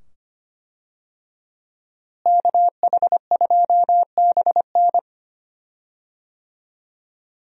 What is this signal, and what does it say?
Morse code, heard as a keyed tone, about 25 wpm, 700 Hz. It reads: KH2BN
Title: USA Call Signs - 5 Characters - Repeated 2x - 25wpm